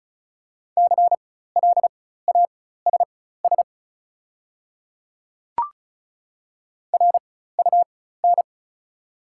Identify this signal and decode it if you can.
Morse code, heard as a keyed tone, about 36 wpm, spaced as for 16 wpm, 690 Hz. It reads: CLASS RUN
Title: 4x Character Spacing - Top 500 Words - 35wpm